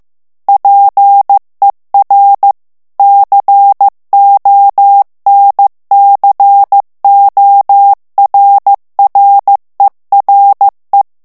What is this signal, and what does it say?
Morse code, heard as a keyed tone, about 15 wpm, 800 Hz. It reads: PER CONCORRERE
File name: morse mp3 / ITALIEN texte clair 2 vitesse 14 mots minute